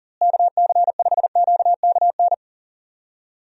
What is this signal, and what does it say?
Morse code, heard as a keyed tone, about 40 wpm, 700 Hz. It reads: KC5QKN
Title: USA Call Signs - 6 Characters - Repeated 3x - 40wpm